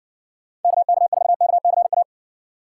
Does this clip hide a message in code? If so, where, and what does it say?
Yes: Morse code from 0.6 to 2.0 s, heard as a keyed tone, about 60 wpm, 700 Hz. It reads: KK4KCA